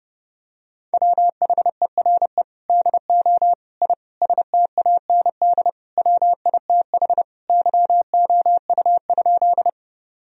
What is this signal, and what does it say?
Morse code, heard as a keyed tone, about 32 wpm, 700 Hz. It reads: WHERE DO I STAND WITH YOU?
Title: Sentences from Top 1000 Words - Review Part 1 - 30wpm